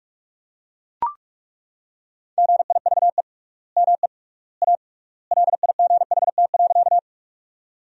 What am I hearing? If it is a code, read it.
Morse code, heard as a keyed tone, about 41 wpm, 700 Hz. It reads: GIVE ME A LIGHT.
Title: 3x Word-Spacing - Sentences from Top 200 Words - 45wpm